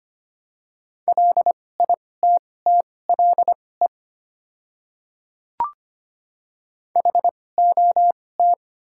Morse code, heard as a keyed tone, about 25 wpm, spaced as for 18 wpm, 700 Hz. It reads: LITTLE HOT
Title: 2x Character Spacing - Top 300 Words - 25wpm